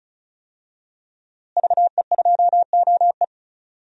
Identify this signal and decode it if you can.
Morse code, heard as a keyed tone, about 34 wpm, 700 Hz. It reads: VE2OE
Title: DXCC-100 Call Signs - 5 Characters - Repeated 2x - 35wpm